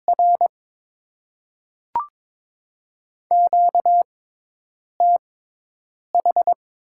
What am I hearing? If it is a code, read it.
Morse code, heard as a keyed tone, about 22 wpm, spaced as for 8 wpm, 690 Hz. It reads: R QTH
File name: Top 20 QSO Elements - 6x Character Spacing - 22wpm